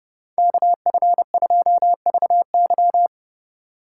Morse code, heard as a keyed tone, about 30 wpm, 700 Hz. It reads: KF2VY
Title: USA Call Signs - 5 Characters - 2x Character Spacing - 30wpm